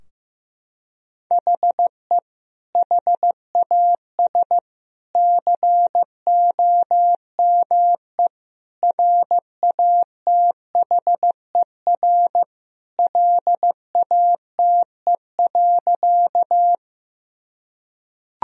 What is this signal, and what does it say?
Morse code, heard as a keyed tone, about 15 wpm, 700 Hz. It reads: HE HAS COME RATHER LATE.
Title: Sentences from Top 1000 Words - Review Part 4 - 15wpm